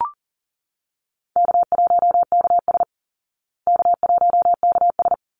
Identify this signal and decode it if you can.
Morse code, heard as a keyed tone, about 40 wpm, 700 Hz. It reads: K1KS K1KS
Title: Easy Call Signs Repeated 2x - 40wpm